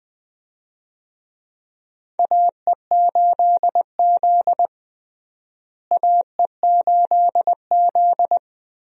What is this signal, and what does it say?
Morse code, heard as a keyed tone, about 20 wpm, 700 Hz. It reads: AE8Z AE8Z
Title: USA Call Signs - 4 Characters - Repeated 3x - 20wpm